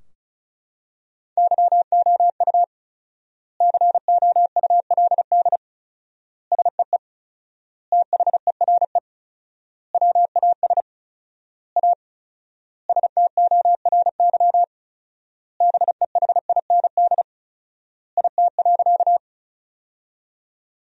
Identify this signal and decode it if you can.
Morse code, heard as a keyed tone, about 36 wpm, 700 Hz. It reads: YOU COULD SEE THERE WAS A STORY BEHIND IT.
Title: 4x Word-Spacing - Sentences from Top 500 Words - 35wpm